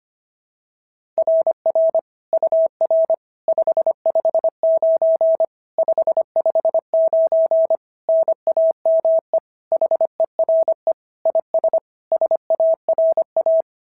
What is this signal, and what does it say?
Morse code, heard as a keyed tone, about 25 wpm, 690 Hz. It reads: RR UR 559 559 NAME HERE IS SARA
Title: SOTA QSO Practice 25wpm